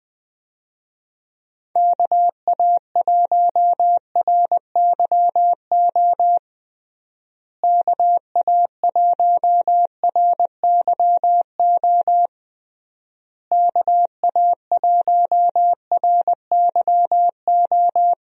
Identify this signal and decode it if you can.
Morse code, heard as a keyed tone, about 21 wpm, 700 Hz. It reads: KA1RYO KA1RYO KA1RYO
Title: USA Call Signs - 6 Characters - Repeated 3x - 20wpm